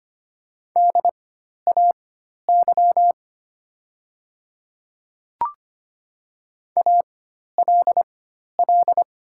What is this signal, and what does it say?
Morse code, heard as a keyed tone, about 25 wpm, spaced as for 12 wpm, 700 Hz. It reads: DAY ALL
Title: 4x Character Spacing - Top 200 Words - 25wpm